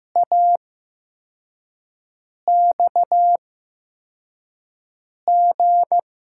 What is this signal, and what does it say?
Morse code, heard as a keyed tone, about 15 wpm, spaced as for 4 wpm, 700 Hz. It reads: AXG
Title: USA Call Signs - 5 Characters - 8x Character Spacing - 15wpm